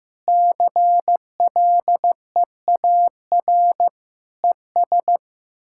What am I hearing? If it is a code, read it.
Morse code, heard as a keyed tone, about 15 wpm, 700 Hz. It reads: CLEAR ES
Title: Practice QSO - Exchange 2 - Info - 15wpm